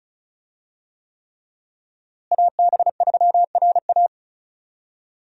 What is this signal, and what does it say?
Morse code, heard as a keyed tone, about 36 wpm, 700 Hz. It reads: AB3RA
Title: USA Call Signs - 5 Characters - Repeated 3x - 35wpm